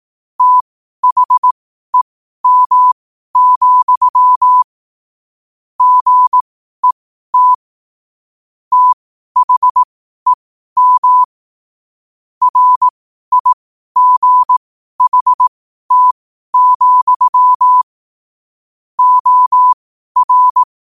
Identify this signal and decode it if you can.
Morse code, heard as a keyed tone, about 18 wpm, spaced as for 12 wpm, 1000 Hz. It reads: THEM, GET THEM RIGHT, OR
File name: qotdfiles / 12wpm